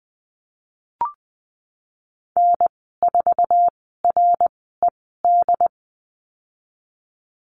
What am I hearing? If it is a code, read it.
Morse code, heard as a keyed tone, about 20 wpm, spaced as for 15 wpm, 700 Hz. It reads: N4RED
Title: 154 Lesson 25g - Letter p - Call Signs 20wpm